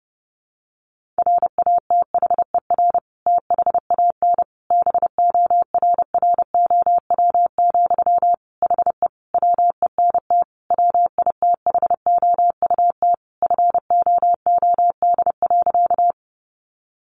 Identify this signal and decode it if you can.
Morse code, heard as a keyed tone, about 30 wpm, 700 Hz. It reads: RATHER THAN BORROW, HE WENT WITHOUT FOOD.
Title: Sentences from Top 1500 Words 30wpm